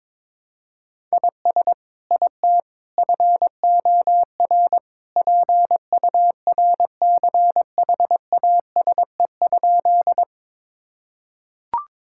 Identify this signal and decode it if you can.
Morse code, heard as a keyed tone, about 22 wpm, 700 Hz. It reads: IS IT FOR PURCHASE?
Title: Sentences from Top 1000 Words - Review Part 5 - 22wpm